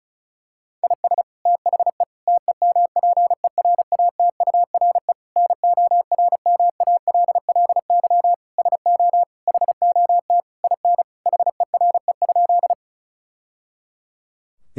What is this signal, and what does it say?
Morse code, heard as a keyed tone, about 36 wpm, 700 Hz. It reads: IS THE TEMPERATURE NORMALLY SO HOT IN HERE?
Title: Sentences from Top 1000 Words - Review Part 6 - 35wpm